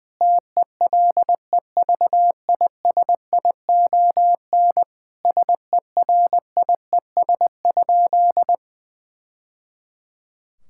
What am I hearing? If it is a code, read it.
Morse code, heard as a keyed tone, about 20 wpm, 700 Hz. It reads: TELEVISION SERIES?
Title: Sentences from Top 1000 Words - Review Part 5 - 20wpm